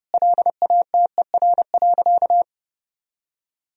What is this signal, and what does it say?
Morse code, heard as a keyed tone, about 30 wpm, 700 Hz. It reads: LATER.
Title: Sentences from Top 1000 Words - Review Part 4 - 30wpm